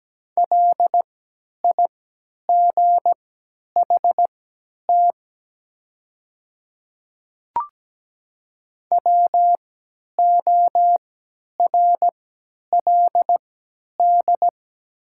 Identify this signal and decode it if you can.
Morse code, heard as a keyed tone, about 17 wpm, spaced as for 10 wpm, 700 Hz. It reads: LIGHT WORLD
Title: 3x Character Spacing - Top 300 Words - 17wpm